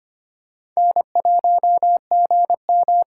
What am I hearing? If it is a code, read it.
Morse code, heard as a keyed tone, about 26 wpm, 700 Hz. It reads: N1GM
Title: USA Callsign DE Callsign - 4 Characters - 25wpm